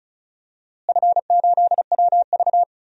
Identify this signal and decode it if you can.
Morse code, heard as a keyed tone, about 34 wpm, 700 Hz. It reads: F8WV
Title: DXCC-100 Call Signs - 4 Characters - Repeated 3x - 35wpm